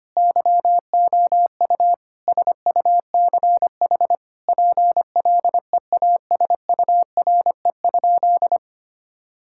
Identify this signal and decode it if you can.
Morse code, heard as a keyed tone, about 25 wpm, 700 Hz. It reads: YOU SUCH PLEASURE?
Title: Sentences from Top 1600 Words 25wpm